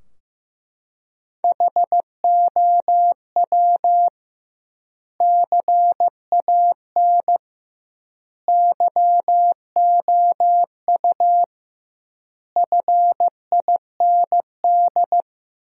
Morse code, heard as a keyed tone, about 15 wpm, 700 Hz. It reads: HOW CAN YOU FIND
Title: 2x Word-Spacing - Sentences from Top 200 Words - 15wpm